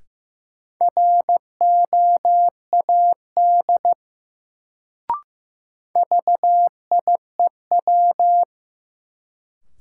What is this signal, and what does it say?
Morse code, heard as a keyed tone, about 15 wpm, 700 Hz. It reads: ROAD VIEW
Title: Sets of 5 Words - New Word Drill - 15wpm